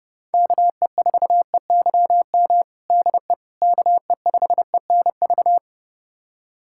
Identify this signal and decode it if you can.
Morse code, heard as a keyed tone, about 32 wpm, 700 Hz. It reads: KE4EYM DE KE5ENV
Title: USA Callsign DE Callsign - 6 Characters - 30wpm